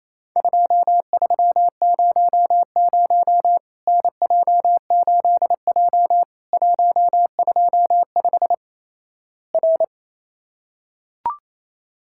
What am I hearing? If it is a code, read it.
Morse code, heard as a keyed tone, about 28 wpm, 700 Hz. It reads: 2300 NJ8J 125 R
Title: Worked All Europe DX Contest - QTC Exchange - 28wpm